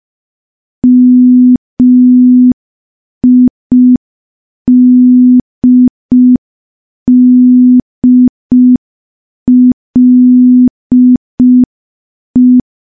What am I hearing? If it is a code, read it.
Morse code, heard as a keyed tone, about 5 wpm, 250 Hz. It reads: MIDDLE
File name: morse / morse-training-05WPM